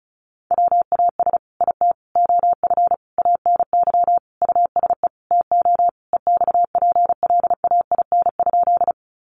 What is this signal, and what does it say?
Morse code, heard as a keyed tone, about 34 wpm, 700 Hz. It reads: WAS IT OF ANY USE TO EXPLAIN?
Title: Sentences from Top 1100 Words 35wpm